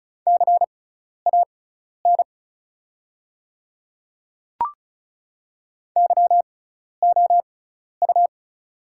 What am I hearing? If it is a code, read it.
Morse code, heard as a keyed tone, about 36 wpm, spaced as for 12 wpm, 700 Hz. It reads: CAN YOU
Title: 6x Character Spacing - Top 100 Words - 35wpm